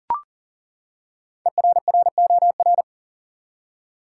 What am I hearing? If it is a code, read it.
Morse code, heard as a keyed tone, about 40 wpm, 700 Hz. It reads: ERROR
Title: Five Letter Words - 40wpm